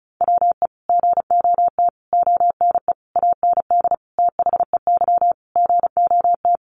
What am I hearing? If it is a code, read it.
Morse code, heard as a keyed tone, about 34 wpm, 700 Hz. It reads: WE GOT ONE AND THEY GOT
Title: Sentences from Top 300 Words 35wpm